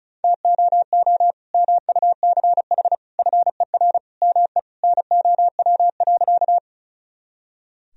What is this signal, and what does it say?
Morse code, heard as a keyed tone, about 36 wpm, 700 Hz. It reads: TOO MUCH FER ME NOW.
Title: Sentences from Top 200 Words and Abbreviations 35wpm